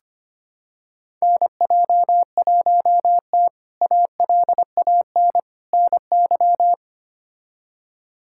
Morse code, heard as a keyed tone, about 25 wpm, 700 Hz. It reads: NJ1T ALAN NY
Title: K1USN SST QSO Practice - 25wpm